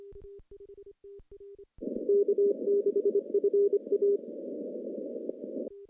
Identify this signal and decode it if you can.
Morse code, heard as a keyed tone, about 24 wpm, 400 Hz. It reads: K6FA
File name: contests / CW-SS-k5tr- -0700